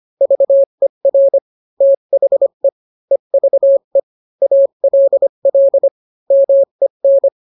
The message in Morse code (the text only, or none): VER THE EVE ALL MEN